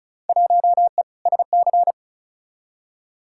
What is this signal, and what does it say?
Morse code, heard as a keyed tone, about 36 wpm, 700 Hz. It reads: 1E SC
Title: Field Day Exchanges - 35wpm